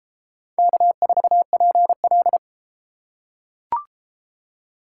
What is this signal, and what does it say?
Morse code, heard as a keyed tone, about 32 wpm, 700 Hz. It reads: K4PL